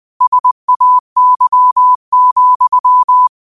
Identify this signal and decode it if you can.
Morse code, heard as a keyed tone, about 20 wpm, 1000 Hz. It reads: SAY,
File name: qotdfiles / 20wpm